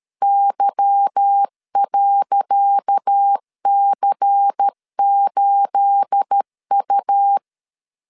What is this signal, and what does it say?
Morse code, heard as a keyed tone, about 13 wpm, 800 Hz. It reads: Y.C8U
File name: LEZIONI / 10 SIGNS / 10C DOT 12wpm 2451